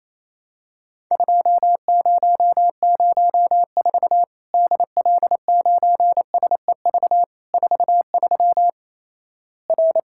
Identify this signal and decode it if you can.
Morse code, heard as a keyed tone, about 28 wpm, 700 Hz. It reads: 2004 DL9SEV 43 R
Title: Worked All Europe DX Contest - QTC Exchange - 28wpm